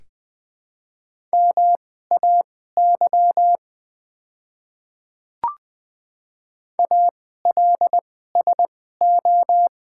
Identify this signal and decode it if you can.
Morse code, heard as a keyed tone, about 20 wpm, spaced as for 14 wpm, 700 Hz. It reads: MAY ALSO
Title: 2x Character Spacing - Top 200 Words - 20wpm